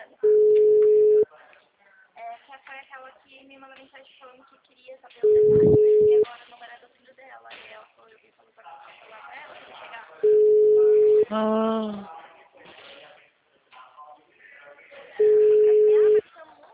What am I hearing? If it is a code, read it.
Ringback tone (Europe, 425 Hz), with speech.